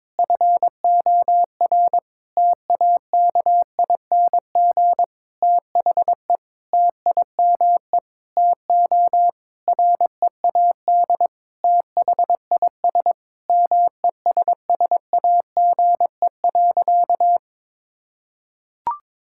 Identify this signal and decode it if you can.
Morse code, heard as a keyed tone, about 22 wpm, 700 Hz. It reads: FOR TAKING THE TIME TO READ THIS MESSAGE.
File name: Sentences with 9-Word Phrases - 22wpm